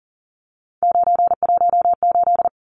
Morse code, heard as a keyed tone, about 40 wpm, 700 Hz. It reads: 918
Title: Numbers - 3-Digit 40wpm